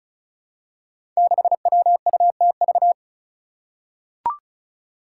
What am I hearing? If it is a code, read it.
Morse code, heard as a keyed tone, about 36 wpm, 700 Hz. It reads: 6WUTV
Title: Five Challenging Letter-Number Combinations 35wpm